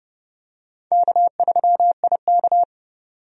Morse code, heard as a keyed tone, about 30 wpm, 700 Hz. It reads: K3IK
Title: USA Call Signs - 4 Characters - 30wpm